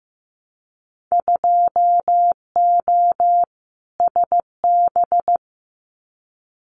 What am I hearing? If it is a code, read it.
Morse code, heard as a keyed tone, about 15 wpm, 700 Hz. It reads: 2O SB
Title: Winter Field Day Exchanges 15wpm